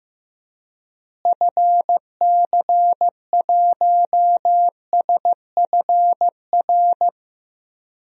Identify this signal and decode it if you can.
Morse code, heard as a keyed tone, about 16 wpm, 700 Hz. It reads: FC1SFR
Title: DXCC-100 Call Signs - 6 Characters - Repeated 3x - 15wpm